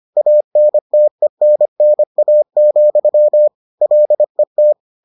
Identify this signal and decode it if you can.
Morse code, heard as a keyed tone, about 25 wpm, 600 Hz. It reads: ANTENNA, LET